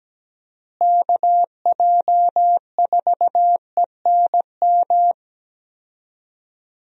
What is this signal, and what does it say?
Morse code, heard as a keyed tone, about 17 wpm, 700 Hz. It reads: KJ4ENM
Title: USA Call Signs - 6 Characters - Repeated 2x - 17wpm